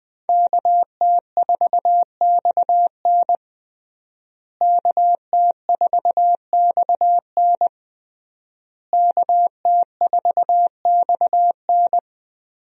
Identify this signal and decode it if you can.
Morse code, heard as a keyed tone, about 21 wpm, 700 Hz. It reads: KT4XN KT4XN KT4XN
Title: USA Call Signs - 5 Characters - Repeated 3x - 20wpm